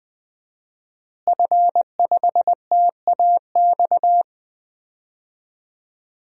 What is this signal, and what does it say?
Morse code, heard as a keyed tone, about 20 wpm, 700 Hz. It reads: F5TAX
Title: DXCC-100 Call Signs - 5 Characters - Repeated 2x - 20wpm